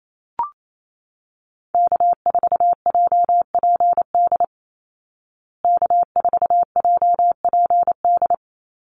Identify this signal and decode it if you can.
Morse code, heard as a keyed tone, about 28 wpm, 700 Hz. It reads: K4JPD K4JPD
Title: Call Signs Repeated 2x - 28wpm